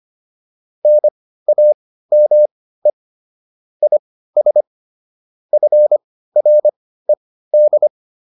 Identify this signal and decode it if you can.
Morse code, heard as a keyed tone, about 25 wpm, spaced as for 15 wpm, 600 Hz. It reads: NAME IS FRED
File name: QSO404 15